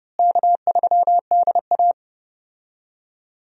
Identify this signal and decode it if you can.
Morse code, heard as a keyed tone, about 30 wpm, 700 Hz. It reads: K3DA
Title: USA Call Signs - 4 Characters - Repeated 3x - 30wpm